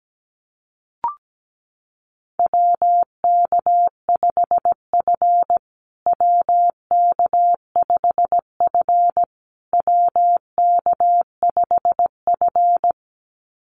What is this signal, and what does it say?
Morse code, heard as a keyed tone, about 17 wpm, 700 Hz. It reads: WK5F WK5F WK5F
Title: USA Call Signs Repeated 3x 17wpm